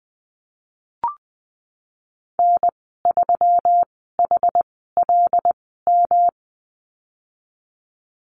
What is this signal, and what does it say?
Morse code, heard as a keyed tone, about 20 wpm, spaced as for 15 wpm, 700 Hz. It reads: N3HLM
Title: 128 Lesson 21g - Number 6 - Call Signs 20wpm